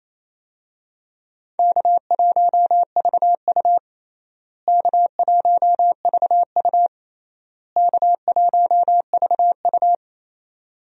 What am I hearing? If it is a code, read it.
Morse code, heard as a keyed tone, about 28 wpm, 700 Hz. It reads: K1VU K1VU K1VU
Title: USA Call Signs - 4 Characters - Repeated 3x - 28wpm